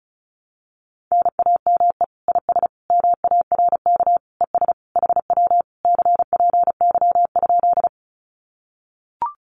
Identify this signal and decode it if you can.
Morse code, heard as a keyed tone, about 34 wpm, 700 Hz. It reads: NAME IS MARK ES HW CPY?
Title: Common QSO Phrases Encore 35wpm